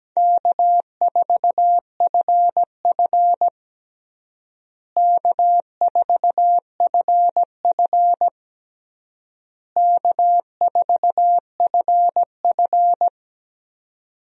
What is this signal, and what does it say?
Morse code, heard as a keyed tone, about 17 wpm, 700 Hz. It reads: K4FF K4FF K4FF
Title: USA Call Signs - 4 Characters - Repeated 3x - 17wpm